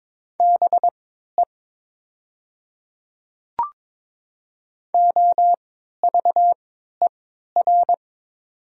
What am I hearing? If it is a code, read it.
Morse code, heard as a keyed tone, about 22 wpm, spaced as for 12 wpm, 690 Hz. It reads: BE OVER
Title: 3x Character Spacing - Top 100 Words - 22wpm